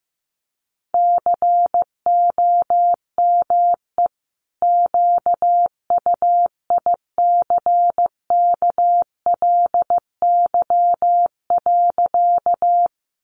Morse code, heard as a keyed tone, about 15 wpm, 700 Hz. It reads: COME QUICKLY.
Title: Sentences from Top 700 Words 15wpm